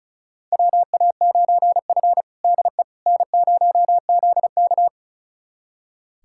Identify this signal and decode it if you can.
Morse code, heard as a keyed tone, about 36 wpm, 700 Hz. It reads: WA9F DE N0ZK
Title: USA Callsign DE Callsign - 4 Characters - 35wpm